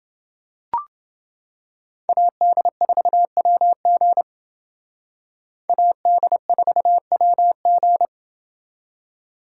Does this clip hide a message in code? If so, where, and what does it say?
Yes: Morse code from 2.1 to 8.1 s, heard as a keyed tone, about 29 wpm, 700 Hz. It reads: AD4WG AD4WG